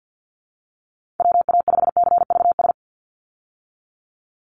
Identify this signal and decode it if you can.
Morse code, heard as a keyed tone, about 50 wpm, 700 Hz. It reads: WA5CVS
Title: USA Call Signs 50wpm